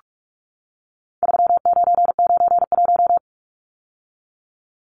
Morse code, heard as a keyed tone, about 44 wpm, 700 Hz. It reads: 3991
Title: Numbers - 4-Digit 45wpm